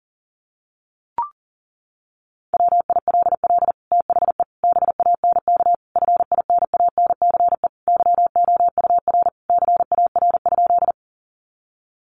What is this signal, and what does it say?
Morse code, heard as a keyed tone, about 40 wpm, 700 Hz. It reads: WILL THE BANK FINANCE YOUR CAR?
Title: Sentences from Top 1500 Words 40wpm